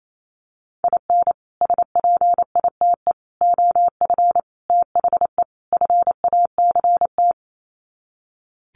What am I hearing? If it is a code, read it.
Morse code, heard as a keyed tone, about 28 wpm, 700 Hz. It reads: IN SPITE OF THE FACT
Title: Sets of 5 Words Encore2 28wpm